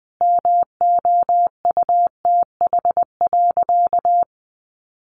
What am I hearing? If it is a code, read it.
Morse code, heard as a keyed tone, about 20 wpm, 700 Hz. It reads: MOUTH.
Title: Sentences from Top 1500 Words 20wpm